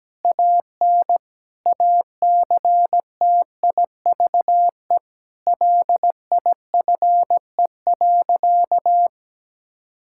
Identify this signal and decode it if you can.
Morse code, heard as a keyed tone, about 17 wpm, 700 Hz. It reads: AN ACTIVE LIFE.
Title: Sentences from Top 1000 Words - Review Part 6 - 17wpm